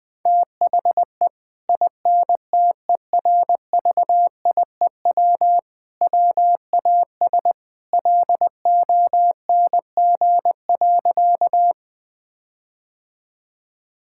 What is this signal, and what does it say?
Morse code, heard as a keyed tone, about 20 wpm, 700 Hz. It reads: THE INTERVIEW WAS LONG.
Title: Sentences from Top 1600 Words 20wpm